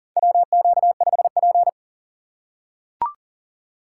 Morse code, heard as a keyed tone, about 40 wpm, 700 Hz. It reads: WQ5P
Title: USA Call Signs - 4 Characters - 8x Character Spacing - 40wpm